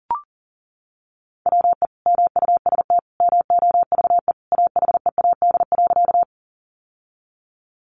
Morse code, heard as a keyed tone, about 40 wpm, 700 Hz. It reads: WE MUST MOVE AHEAD.
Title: Sentences from Top 1100 Words 40wpm